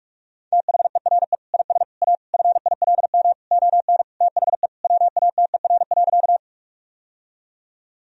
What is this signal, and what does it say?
Morse code, heard as a keyed tone, about 47 wpm, 700 Hz. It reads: THERE IS A FILM ON THE WATER.
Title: Sentences from Top 1000 Words - Review Part 3 - 45wpm